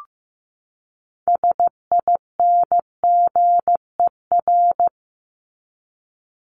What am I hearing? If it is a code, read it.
Morse code, heard as a keyed tone, about 15 wpm, 700 Hz. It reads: SINGER
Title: Top 1501-1750 Words 15wpm